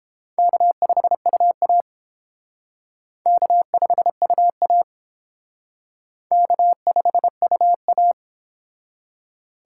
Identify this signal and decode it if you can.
Morse code, heard as a keyed tone, about 29 wpm, 700 Hz. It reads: K5UA K5UA K5UA